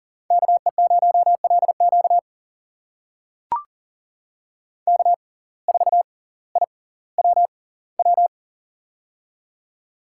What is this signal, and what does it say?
Morse code, heard as a keyed tone, about 39 wpm, spaced as for 20 wpm, 700 Hz. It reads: KE0LQ K4IWW
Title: USA Call Signs - 5 Characters - 6x Character Spacing - 40wpm